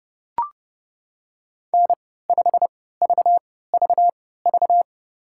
Morse code, heard as a keyed tone, about 30 wpm, spaced as for 17 wpm, 700 Hz. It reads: N5VVV
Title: USA Call Signs - 5 Characters - 3x Character Spacing - 30wpm